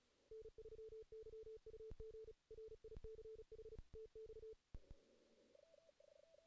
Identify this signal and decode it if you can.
Morse code, heard as a keyed tone, about 34 wpm, 440 Hz. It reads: N3YUG RICH TX
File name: 2016 JAN CW NAQP / 2016 Jan naqp k5ot - 0-2